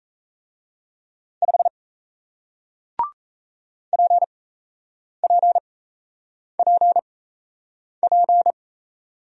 Morse code, heard as a keyed tone, about 37 wpm, spaced as for 8 wpm, 700 Hz. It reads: 5 PPPP